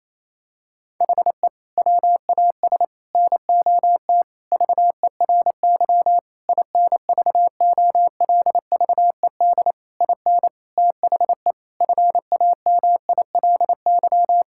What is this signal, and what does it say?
Morse code, heard as a keyed tone, about 28 wpm, 700 Hz. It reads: HE WAS NOT VERY INVOLVED IN THE FAMILY
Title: Sentences from Top 1000 Words - Review Part 5 - 28wpm